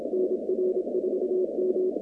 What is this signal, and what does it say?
Morse code, heard as a keyed tone, about 26 wpm, 370 Hz, with noise. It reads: NR4M
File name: 2019-cw-wpx-kt5j-k5pi